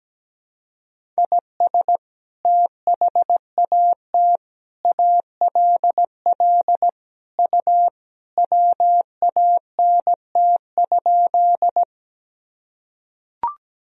Morse code, heard as a keyed tone, about 17 wpm, 700 Hz. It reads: IS THAT ALL U WANT?
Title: Sentences from Top 200 Words and Abbreviations 17wpm